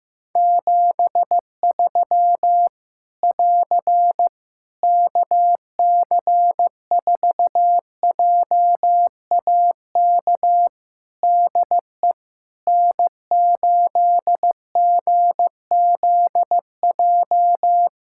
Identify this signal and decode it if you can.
Morse code, heard as a keyed tone, about 15 wpm, 700 Hz. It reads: 73 <AR> KC4JAK DE N8GZJ
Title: Practice QSO - Exchange 4 - Ending - 15wpm